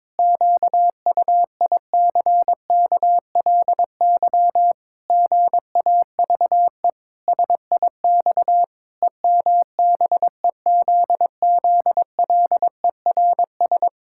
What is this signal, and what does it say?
Morse code, heard as a keyed tone, about 22 wpm, 700 Hz. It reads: QUICKLY GAVE SIX EMBEZZLERS
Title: Pangram Challenge v2 - 22wpm